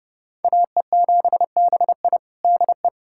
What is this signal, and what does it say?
Morse code, heard as a keyed tone, about 30 wpm, 700 Hz. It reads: AE7BI DE
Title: USA Callsign DE Callsign 30wpm